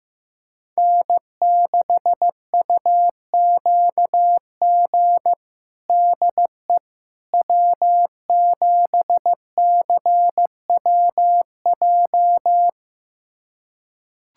Morse code, heard as a keyed tone, about 15 wpm, 700 Hz. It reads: N6UQG DE W7CWJ
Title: USA Callsign DE Callsign - 5 Characters - 15wpm